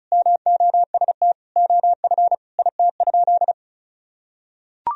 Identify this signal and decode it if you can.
Morse code, heard as a keyed tone, about 36 wpm, 700 Hz. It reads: MOST OF IT?
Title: Sentences from Top 1000 Words - Review Part 1 - 35wpm